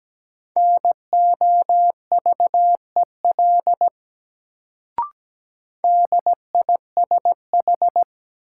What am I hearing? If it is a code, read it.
Morse code, heard as a keyed tone, about 17 wpm, 700 Hz. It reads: NOVEL DISH
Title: Sentences from Top 1300 Words - New Word Drill Alt - 17wpm